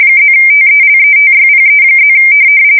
RTTY, about 45.45 baud. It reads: +TELETYPE+
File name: fsk170 / morkrum